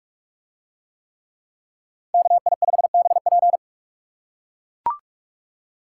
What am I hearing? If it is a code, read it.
Morse code, heard as a keyed tone, about 41 wpm, 700 Hz. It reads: KI5BP